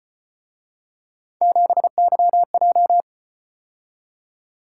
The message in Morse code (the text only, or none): 7YJ